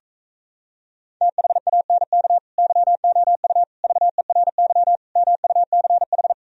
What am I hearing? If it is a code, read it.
Morse code, heard as a keyed tone, about 37 wpm, 700 Hz. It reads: THANK YOU VERY MUCH